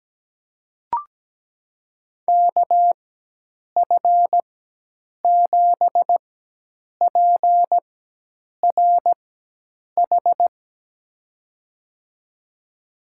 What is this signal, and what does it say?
Morse code, heard as a keyed tone, about 17 wpm, spaced as for 8 wpm, 700 Hz. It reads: KF7PRH
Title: USA Call Signs - 4x Character Spacing - 17wpm